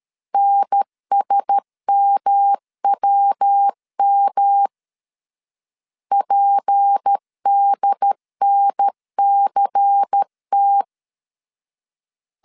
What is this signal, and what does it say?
Morse code, heard as a keyed tone, about 13 wpm, 800 Hz. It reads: NSMWM PDNCT
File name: CW / DET / 1 6 1500C S B 12wpm 3431